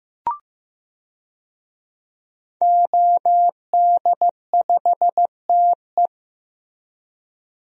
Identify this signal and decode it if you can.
Morse code, heard as a keyed tone, about 16 wpm, 700 Hz. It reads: OD5TE
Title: DXCC-100 Call Signs - 5 Characters - Repeated 2x - 15wpm